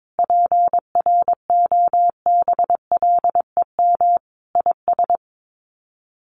Morse code, heard as a keyed tone, about 22 wpm, 700 Hz. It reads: PROBLEM IS
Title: Sets of 5 Words Encore2 22wpm